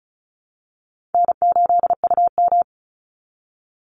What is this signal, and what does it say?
Morse code, heard as a keyed tone, about 34 wpm, 700 Hz. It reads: N8UM
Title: Easy Call Signs Repeated 2x - 35wpm